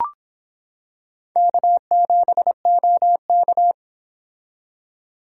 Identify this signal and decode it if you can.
Morse code, heard as a keyed tone, about 26 wpm, 700 Hz. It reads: K7OK